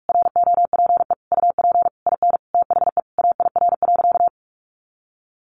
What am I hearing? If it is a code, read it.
Morse code, heard as a keyed tone, about 44 wpm, 700 Hz. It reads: ROPE UP IN THE AIR.
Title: Easy Phrases 45wpm